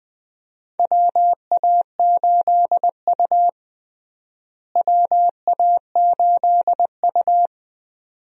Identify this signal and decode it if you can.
Morse code, heard as a keyed tone, about 20 wpm, 700 Hz. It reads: WA8U WA8U
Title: USA Call Signs - 4 Characters - Repeated 2x - 20wpm